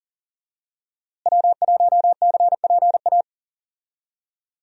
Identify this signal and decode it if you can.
Morse code, heard as a keyed tone, about 40 wpm, 700 Hz. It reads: W1CPA
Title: USA Call Signs - 5 Characters - Repeated 2x - 40wpm